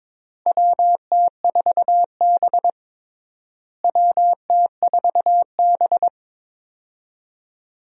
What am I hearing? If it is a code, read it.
Morse code, heard as a keyed tone, about 22 wpm, 700 Hz. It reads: WT4B WT4B
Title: USA Call Signs - 4 Characters - Repeated 3x - 22wpm